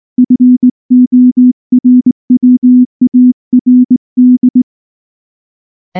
Morse code, heard as a keyed tone, about 19 wpm, 250 Hz. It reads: FORWARD
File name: morse-training-20WPM